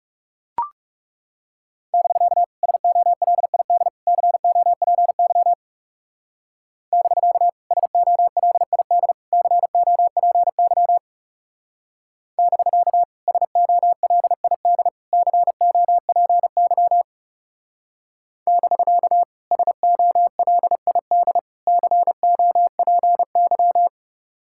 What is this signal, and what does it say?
Morse code, heard as a keyed tone, about 38 wpm, 700 Hz. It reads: <BK> SOLID COPY <BK> SOLID COPY <BK> SOLID COPY <BK> SOLID COPY